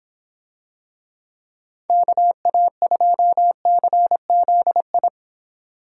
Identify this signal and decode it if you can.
Morse code, heard as a keyed tone, about 24 wpm, 700 Hz. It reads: KA2CZI